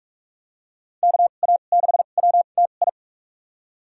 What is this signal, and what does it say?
Morse code, heard as a keyed tone, about 46 wpm, spaced as for 33 wpm, 700 Hz. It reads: KA6WTI
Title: USA Call Signs - 6 Characters - 2x Character Spacing - 45wpm